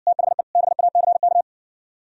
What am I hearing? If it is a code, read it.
Morse code, heard as a keyed tone, about 57 wpm, 700 Hz. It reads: THE BACK